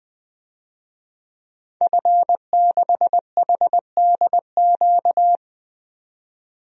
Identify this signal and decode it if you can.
Morse code, heard as a keyed tone, about 20 wpm, 700 Hz. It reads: F6HDQ
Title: DXCC-100 Call Signs - 5 Characters - 20wpm